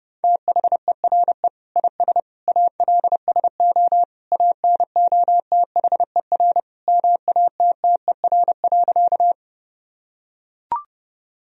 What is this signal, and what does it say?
Morse code, heard as a keyed tone, about 30 wpm, 700 Hz. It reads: THERE IS ALSO ANOTHER MATTER.
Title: Sentences from Top 1000 Words - Review Part 1 - 30wpm